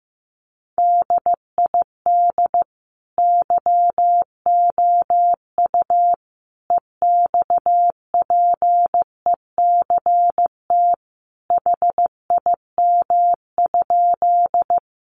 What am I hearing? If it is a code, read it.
Morse code, heard as a keyed tone, about 15 wpm, 700 Hz. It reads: DID YOU EXPECT HIM?
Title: Sentences from Top 1100 Words 15wpm